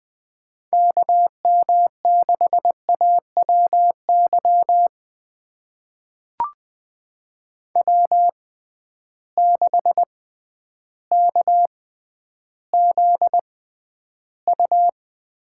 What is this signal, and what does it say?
Morse code, heard as a keyed tone, about 21 wpm, spaced as for 11 wpm, 700 Hz. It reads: KM6AWY W6KZU
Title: USA Call Signs - 6x Character Spacing - 20wpm